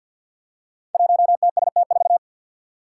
Morse code, heard as a keyed tone, about 53 wpm, 700 Hz. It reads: 1TST4
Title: Five Challenging Letter-Number Combinations 50wpm